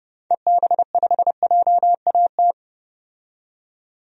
Morse code, heard as a keyed tone, about 32 wpm, 700 Hz. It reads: EB5JAT
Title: DXCC-100 Call Signs - 6 Characters - 30wpm